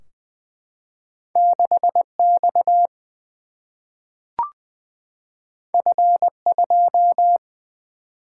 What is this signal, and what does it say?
Morse code, heard as a keyed tone, about 20 wpm, 700 Hz. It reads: 6X F2
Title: Single-Letter Single-Number 20wpm